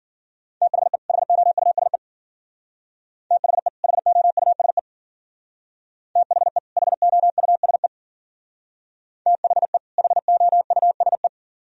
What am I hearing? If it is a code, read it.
Morse code, heard as a keyed tone, about 48 wpm, 690 Hz. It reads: THE HOUSE THE HOUSE THE HOUSE THE HOUSE
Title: Speed-Racing - Sets of 2 Words 40wpm